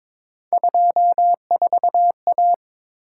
Morse code, heard as a keyed tone, about 26 wpm, 700 Hz. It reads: 24A
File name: Speed-Racing - Three Letter-Number Combinations - 22wpm